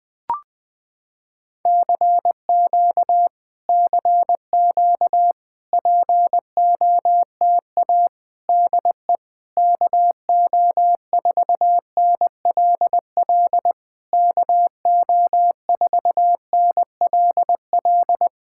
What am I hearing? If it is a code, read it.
Morse code, heard as a keyed tone, about 20 wpm, 700 Hz. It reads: CQ CQ POTA DE KO4NLL KO4NLL
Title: POTA QSO Practice - 20wpm